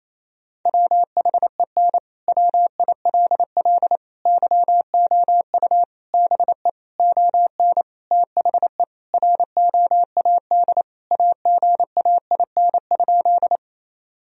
Morse code, heard as a keyed tone, about 28 wpm, 700 Hz. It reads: WHEN WILL YOU BE ON THE ROAD AGAIN?
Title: Sentences from Top 1000 Words - Review Part 2 - 28wpm